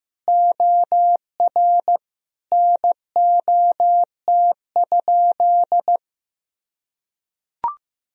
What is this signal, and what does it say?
Morse code, heard as a keyed tone, about 15 wpm, 700 Hz. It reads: OR NOT?
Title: Sentences from Top 1000 Words - Review Part 3 - 15wpm